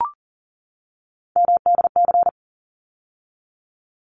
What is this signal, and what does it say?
Morse code, heard as a keyed tone, about 40 wpm, 700 Hz. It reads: MDC
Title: ARRL Sections 40wpm